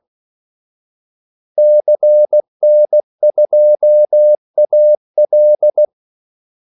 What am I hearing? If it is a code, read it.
Morse code, heard as a keyed tone, about 16 wpm, 600 Hz. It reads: CN2AL
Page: Kaldesignaler 16 wpm